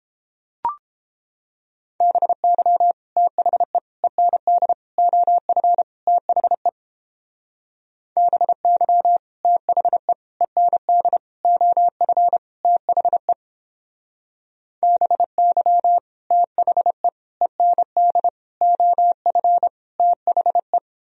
Morse code, heard as a keyed tone, about 29 wpm, 700 Hz. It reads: BY THE END OF THE BY THE END OF THE BY THE END OF THE